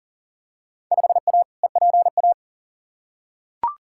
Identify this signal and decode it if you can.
Morse code, heard as a keyed tone, about 37 wpm, 700 Hz. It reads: 5A EPA
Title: Field Day Exchanges - 40wpm